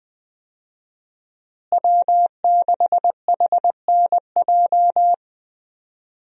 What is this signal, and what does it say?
Morse code, heard as a keyed tone, about 20 wpm, 700 Hz. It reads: W6HNJ
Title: USA Call Signs - 5 Characters - Repeated 3x - 20wpm